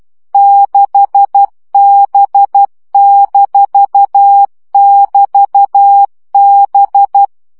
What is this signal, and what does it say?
Morse code, heard as a keyed tone, about 12 wpm, 800 Hz. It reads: 6B-<BT>B
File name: CW gyakorlas b 6 egyenlo minusz 10 baud